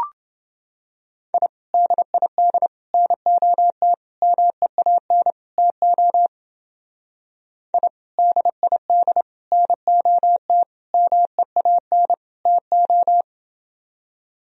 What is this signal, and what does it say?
Morse code, heard as a keyed tone, about 29 wpm, 700 Hz. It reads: I DID NOT MEAN TO I DID NOT MEAN TO